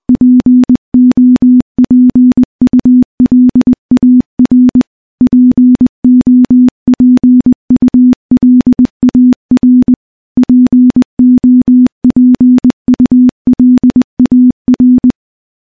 Morse code, heard as a keyed tone, about 20 wpm, 250 Hz. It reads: POPULAR POPULAR POPULAR
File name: morse / morse-training-20WPM